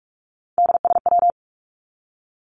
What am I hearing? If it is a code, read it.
Morse code, heard as a keyed tone, about 44 wpm, 700 Hz. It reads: DSW
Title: Abbreviations 45wpm